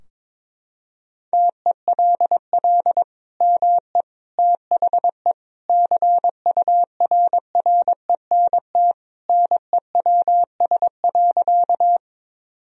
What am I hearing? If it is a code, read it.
Morse code, heard as a keyed tone, about 22 wpm, 700 Hz. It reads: TELL ME THE CURRENT NEWS.
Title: Sentences from Top 1000 Words - Review Part 4 - 22wpm